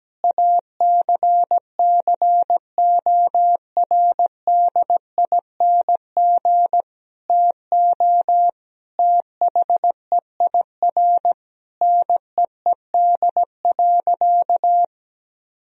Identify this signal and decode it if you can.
Morse code, heard as a keyed tone, about 17 wpm, 700 Hz. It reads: ACCORDING TO THEIR NEED.
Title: Sentences from Top 1000 Words - Review Part 4 - 17wpm